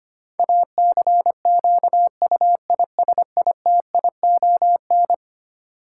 Morse code, heard as a keyed tone, about 25 wpm, 700 Hz. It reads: ACQUISITION
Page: morse-code-ninja-practice-25wpm